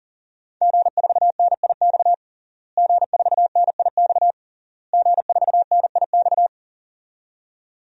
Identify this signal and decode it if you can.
Morse code, heard as a keyed tone, about 40 wpm, 700 Hz. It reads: G4NIX G4NIX G4NIX
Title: DXCC-100 Call Signs - 5 Characters - Repeated 3x - 40wpm